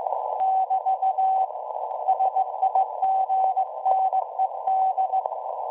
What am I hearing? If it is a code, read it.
Morse code, heard as a keyed tone, about 17 wpm, 760 Hz. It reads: <BT> SIGNED
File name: WebSDR records / SAQ at PI4THT part od 1005z 17-2kHz